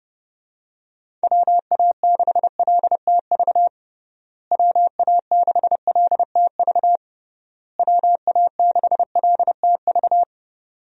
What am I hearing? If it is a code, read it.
Morse code, heard as a keyed tone, about 30 wpm, 700 Hz. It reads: WA6LTV WA6LTV WA6LTV
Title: USA Call Signs - 6 Characters - Repeated 3x - 30wpm